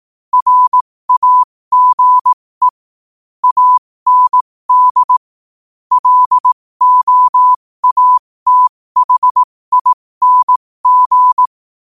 Morse code, heard as a keyed tone, about 18 wpm, spaced as for 15 wpm, 1000 Hz. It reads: RAGE AND LOATHING
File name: qotdfiles / 15wpm